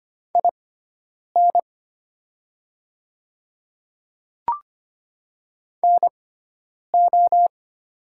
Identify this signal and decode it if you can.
Morse code, heard as a keyed tone, about 25 wpm, spaced as for 9 wpm, 700 Hz. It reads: IN NO
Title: 6x Character Spacing - Top 100 Words - 25wpm